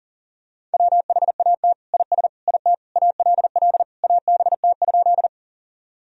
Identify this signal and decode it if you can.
Morse code, heard as a keyed tone, about 40 wpm, 700 Hz. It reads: WHAT IS IT ALL ABT?
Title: Sentences from Top 100 Words and Abbreviations 40wpm